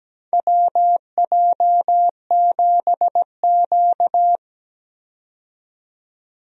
Morse code, heard as a keyed tone, about 17 wpm, 700 Hz. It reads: WJ7Q
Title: USA Call Signs - 4 Characters - 17wpm